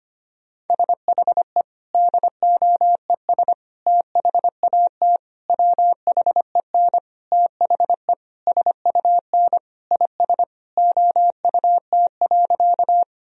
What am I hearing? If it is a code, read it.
Morse code, heard as a keyed tone, about 25 wpm, 700 Hz. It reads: SHE DOES THAT WHEN THE SUN IS OUT.
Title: Sentences from Top 1000 Words - Review Part 1 - 25wpm